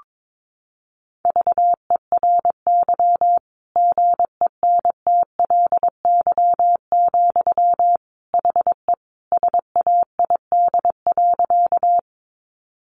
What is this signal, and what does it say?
Morse code, heard as a keyed tone, about 22 wpm, 700 Hz. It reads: VERY GENTLY, HE SAID.
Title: Sentences from Top 1500 Words 22wpm